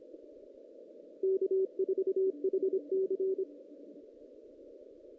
Morse code, heard as a keyed tone, about 26 wpm, 410 Hz. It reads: K4HC
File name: 2016 JAN CW NAQP / 2016 Jan naqp k5ot - 0-5